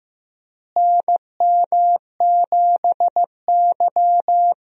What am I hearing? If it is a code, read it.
Morse code, heard as a keyed tone, about 15 wpm, 700 Hz. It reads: NM7Y
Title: USA Call Signs - 4 Characters - Repeated 2x - 15wpm